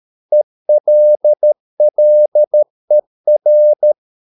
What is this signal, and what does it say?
Morse code, heard as a keyed tone, about 13 wpm, 600 Hz. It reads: ELLER
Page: Ferie 13 wpm